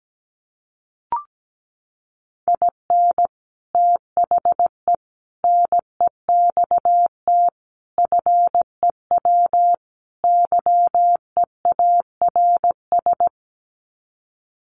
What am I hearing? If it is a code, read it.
Morse code, heard as a keyed tone, about 17 wpm, 700 Hz. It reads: IN THE NEXT FEW YEARS
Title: Sets of 5 Words Encore 17wpm